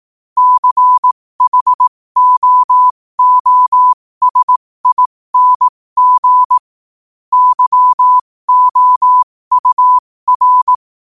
Morse code, heard as a keyed tone, about 18 wpm, spaced as for 15 wpm, 1000 Hz. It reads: CHOOSING YOUR
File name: qotdfiles / 15wpm